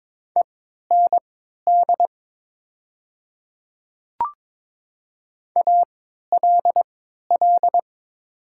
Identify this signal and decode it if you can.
Morse code, heard as a keyed tone, about 22 wpm, spaced as for 13 wpm, 700 Hz. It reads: END ALL
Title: 3x Character Spacing - Top 200 Words - 22wpm